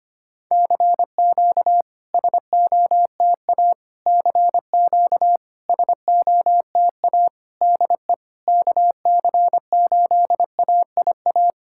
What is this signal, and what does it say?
Morse code, heard as a keyed tone, about 25 wpm, 700 Hz. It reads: CQ SOTA CQ SOTA DE KC8AIA
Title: SOTA QSO Practice 25wpm